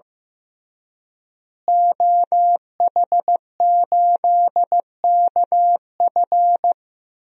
Morse code, heard as a keyed tone, about 15 wpm, 700 Hz. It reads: OH8KF
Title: DXCC-100 Call Signs - 5 Characters - Repeated 3x - 15wpm